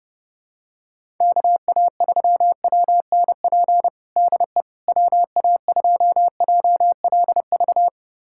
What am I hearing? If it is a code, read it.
Morse code, heard as a keyed tone, about 32 wpm, 700 Hz. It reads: KA3WNP DE WA2JLV
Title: USA Callsign DE Callsign - 6 Characters - 30wpm